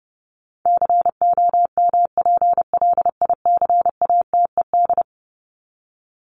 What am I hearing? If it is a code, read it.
Morse code, heard as a keyed tone, about 30 wpm, 700 Hz. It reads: COMPLICATED
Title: Top 1001-1250 Words 30wpm